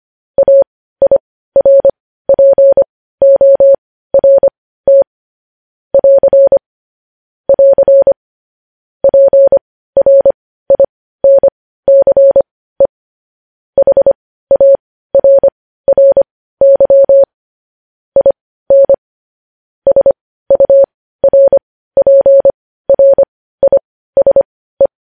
Morse code, heard as a keyed tone, about 25 wpm, spaced as for 15 wpm, 550 Hz. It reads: AIRPORT <AR> <AR> PRINCE HARRY IN SURPRISE
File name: podcast / News headlines-1x-25x15